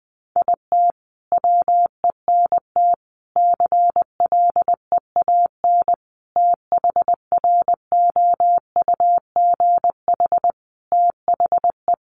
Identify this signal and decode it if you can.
Morse code, heard as a keyed tone, about 20 wpm, 700 Hz. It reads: IT WENT CLEAN THROUGH THE
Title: Sentences from Top 1200 Words 20wpm